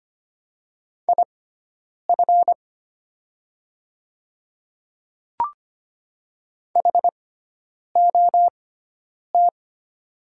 Morse code, heard as a keyed tone, about 25 wpm, spaced as for 9 wpm, 700 Hz. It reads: IF HOT
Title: 6x Character Spacing - Top 300 Words - 25wpm